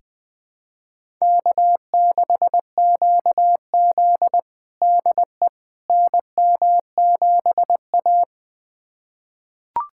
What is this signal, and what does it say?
Morse code, heard as a keyed tone, about 20 wpm, 700 Hz. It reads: K6QZ DE NM7A
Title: USA Callsign DE Callsign - 4 Characters - 20wpm